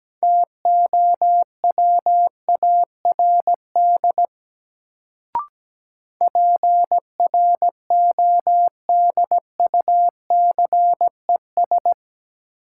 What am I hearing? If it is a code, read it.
Morse code, heard as a keyed tone, about 17 wpm, 700 Hz. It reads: TOWARD PRODUCES
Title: Sentences from Top 1000 Words - Review Part 6 - New Word Drill - 17wpm